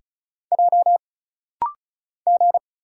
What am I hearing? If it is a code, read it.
Morse code, heard as a keyed tone, about 34 wpm, 700 Hz. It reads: J G
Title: Single Letters - Rapid-Fire 35wpm